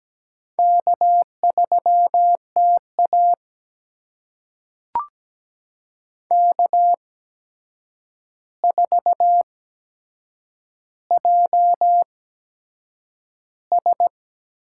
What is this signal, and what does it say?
Morse code, heard as a keyed tone, about 17 wpm, spaced as for 7 wpm, 700 Hz. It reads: K3TA K4JS
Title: USA Call Signs - 4 Characters - 8x Character Spacing - 17wpm